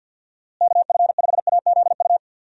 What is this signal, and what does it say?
Morse code, heard as a keyed tone, about 50 wpm, 700 Hz. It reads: KF5AZU
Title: USA Call Signs - 6 Characters - Repeated 2x - 50wpm